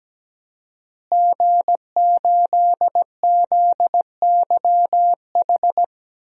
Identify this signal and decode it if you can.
Morse code, heard as a keyed tone, about 17 wpm, 700 Hz. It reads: G8ZYH
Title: DXCC-100 Call Signs - 5 Characters - Repeated 3x - 17wpm